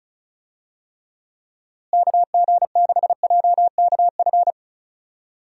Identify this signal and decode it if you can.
Morse code, heard as a keyed tone, about 34 wpm, 700 Hz. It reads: KG6JKF
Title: USA Call Signs - 6 Characters - Repeated 2x - 35wpm